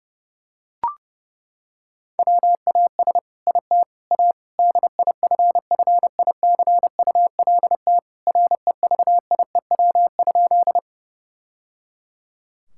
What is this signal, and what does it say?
Morse code, heard as a keyed tone, about 30 wpm, 700 Hz. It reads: WAS IT A DIFFICULT REVIEW?
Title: Sentences from Top 1000 Words - Review Part 6 - 30wpm